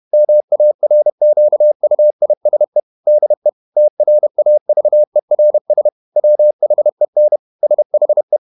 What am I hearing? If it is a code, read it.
Morse code, heard as a keyed tone, about 31 wpm, 600 Hz. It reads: MARQUISE DE TRAVERS WHEN SHE
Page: Never 31wpm | CW med Gnister